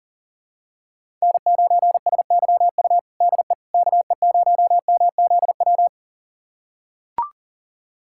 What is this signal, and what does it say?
Morse code, heard as a keyed tone, about 40 wpm, 700 Hz. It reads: N9SYU DE KE0MZW
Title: USA Callsign DE Callsign 40wpm